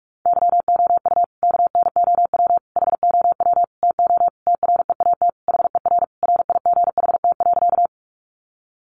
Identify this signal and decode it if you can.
Morse code, heard as a keyed tone, about 44 wpm, 700 Hz. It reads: YOU KNOW HOW TO TREAT HER RIGHT.
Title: Sentences from Top 1300 Words 45wpm